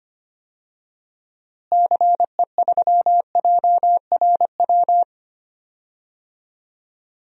Morse code, heard as a keyed tone, about 25 wpm, 700 Hz. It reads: CE3JRW
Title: DXCC-100 Call Signs - 6 Characters - Repeated 2x - 25wpm